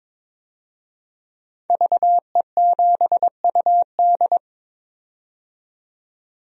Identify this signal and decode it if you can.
Morse code, heard as a keyed tone, about 22 wpm, 700 Hz. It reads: VE7UD
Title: DXCC-100 Call Signs - 5 Characters - 22wpm